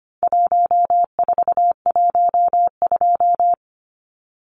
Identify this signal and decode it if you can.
Morse code, heard as a keyed tone, about 25 wpm, 700 Hz. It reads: 1412
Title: Numbers - 4-Digit 25wpm